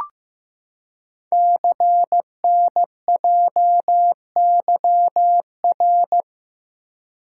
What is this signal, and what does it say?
Morse code, heard as a keyed tone, about 15 wpm, 700 Hz. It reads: CNJYR
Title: Code Groups - 5 Letters - 15wpm